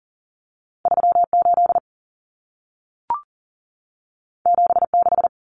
Morse code, heard as a keyed tone, about 40 wpm, 700 Hz. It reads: 38 76
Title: Common Cut Numbers - 2-Digit 40wpm